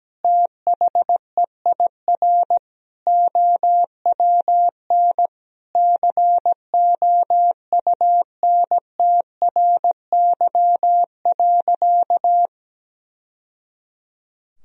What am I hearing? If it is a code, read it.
Morse code, heard as a keyed tone, about 17 wpm, 700 Hz. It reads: THEIR OWN COUNTRY.
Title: Sentences from Top 1000 Words - Review Part 1 - 17wpm